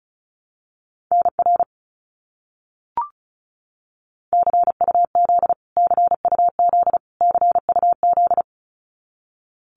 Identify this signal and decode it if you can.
Morse code, heard as a keyed tone, about 34 wpm, 700 Hz. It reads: NR CUZ CUZ CUZ
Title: Abbreviations Repeated 3x - 35wpm